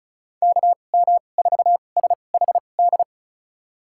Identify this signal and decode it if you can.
Morse code, heard as a keyed tone, about 36 wpm, spaced as for 25 wpm, 700 Hz. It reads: KM4SHD
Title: USA Call Signs - 2x Character Spacing - 35wpm